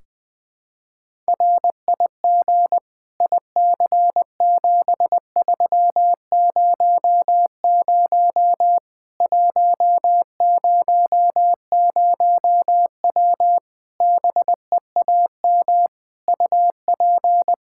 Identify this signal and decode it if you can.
Morse code, heard as a keyed tone, about 20 wpm, 700 Hz. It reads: RIG IC7300 100W BEAM UP
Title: Practice QSO - Exchange 2 - Info - 20wpm